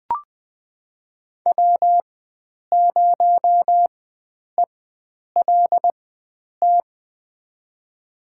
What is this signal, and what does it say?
Morse code, heard as a keyed tone, about 20 wpm, spaced as for 9 wpm, 700 Hz. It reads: W0ELT
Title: USA Call Signs - 4x Character Spacing - 20wpm